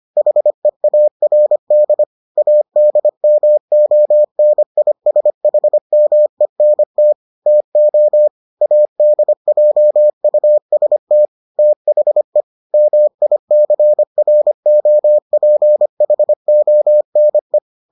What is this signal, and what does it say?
Morse code, heard as a keyed tone, about 25 wpm, 600 Hz. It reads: HEARD ADMONISHMENT TO ADJUST THE MICROPHONE